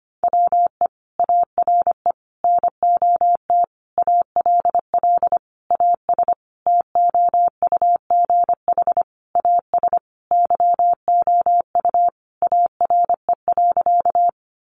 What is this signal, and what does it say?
Morse code, heard as a keyed tone, about 25 wpm, 700 Hz. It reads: WE ARE NOT ALL AS TOUGH AS YOU ARE.
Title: Sentences from Top 1200 Words 25wpm